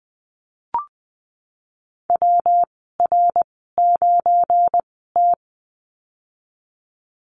Morse code, heard as a keyed tone, about 20 wpm, spaced as for 14 wpm, 700 Hz. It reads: WR9T
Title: 213 Lesson 34g - Number 8 - Call Signs 20wpm